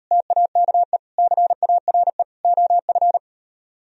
Morse code, heard as a keyed tone, about 36 wpm, 700 Hz. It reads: TAKE CARE OF